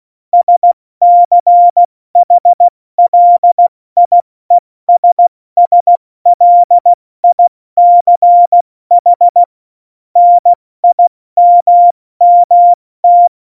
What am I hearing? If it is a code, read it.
Morse code, heard as a keyed tone, about 16 wpm, 700 Hz. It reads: SCHLIESSLICH NIMMT
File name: Morsekurs / Witze / Witz 16. April 2019, 70 F80